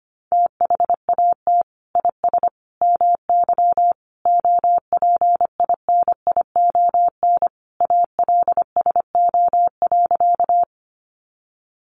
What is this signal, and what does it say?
Morse code, heard as a keyed tone, about 25 wpm, 700 Hz. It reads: THAT IS MY OPINION ALSO.
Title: Sentences from Top 1200 Words 25wpm